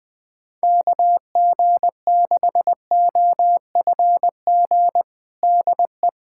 Morse code, heard as a keyed tone, about 21 wpm, 700 Hz. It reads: KG6OFG DE
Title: USA Callsign DE Callsign 20wpm